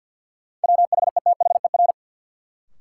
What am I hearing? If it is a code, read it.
Morse code, heard as a keyed tone, about 52 wpm, 700 Hz. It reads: WHETHER
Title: Sentences from Top 1000 Words - Review Part 3 - New Word Drill - 50wpm